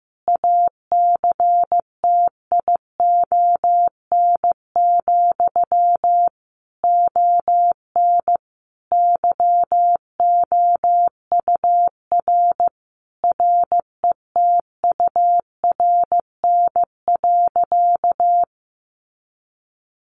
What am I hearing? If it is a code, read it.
Morse code, heard as a keyed tone, about 15 wpm, 700 Hz. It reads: ACTION, ON YOUR RETURN.
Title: A Sound Of Thunder 15wpm